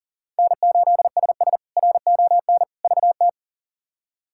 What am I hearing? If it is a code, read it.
Morse code, heard as a keyed tone, about 40 wpm, 700 Hz. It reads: N8SS RON VT
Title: K1USN SST QSO Practice - 40wpm